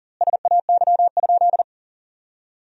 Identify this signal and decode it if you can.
Morse code, heard as a keyed tone, about 37 wpm, 700 Hz. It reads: SAY?
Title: 2x Word-Spacing - Sentences from Top 500 Words - 40wpm